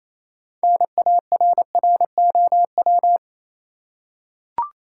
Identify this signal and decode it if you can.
Morse code, heard as a keyed tone, about 28 wpm, 700 Hz. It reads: NARROW
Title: Six Letter Words - 28wpm